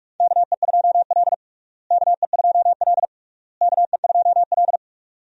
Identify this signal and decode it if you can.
Morse code, heard as a keyed tone, about 46 wpm, 700 Hz. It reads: KE2L KE2L KE2L
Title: USA Call Signs - 4 Characters - Repeated 3x - 45wpm